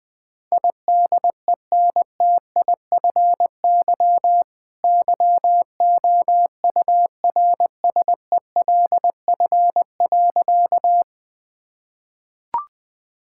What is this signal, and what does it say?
Morse code, heard as a keyed tone, about 20 wpm, 700 Hz. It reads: IDENTIFY YOURSELF.
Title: Sentences from Top 1000 Words - Review Part 6 - 20wpm